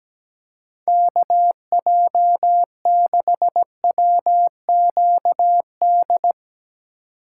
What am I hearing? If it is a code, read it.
Morse code, heard as a keyed tone, about 17 wpm, 700 Hz. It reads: KJ6WQD
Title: USA Call Signs - 6x Character Spacing - 17wpm